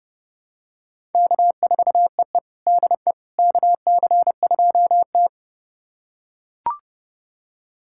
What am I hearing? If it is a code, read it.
Morse code, heard as a keyed tone, about 32 wpm, 700 Hz. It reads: K4EE DE KC2T
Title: USA Callsign DE Callsign - 4 Characters - 30wpm